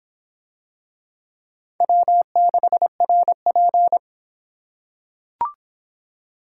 Morse code, heard as a keyed tone, about 26 wpm, 700 Hz. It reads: W6RP